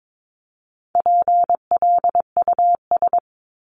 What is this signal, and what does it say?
Morse code, heard as a keyed tone, about 22 wpm, 700 Hz. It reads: PLUS
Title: Top 1001-1250 Words 22wpm